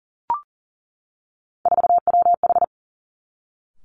Morse code, heard as a keyed tone, about 40 wpm, 700 Hz. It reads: 4WH
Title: Three Challenging Letter-Number Combinations 40wpm